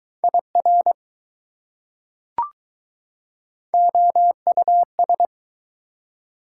Morse code, heard as a keyed tone, about 21 wpm, 700 Hz. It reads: IR OUS